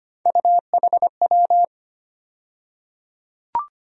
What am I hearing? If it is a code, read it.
Morse code, heard as a keyed tone, about 25 wpm, 700 Hz. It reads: UHW
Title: USA Call Sign Postfixes - 3 Letters 25wpm